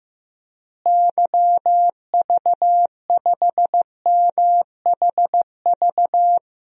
Morse code, heard as a keyed tone, about 15 wpm, 700 Hz. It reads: YV5MHV
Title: DXCC-100 Call Signs - 6 Characters - Repeated 3x - 15wpm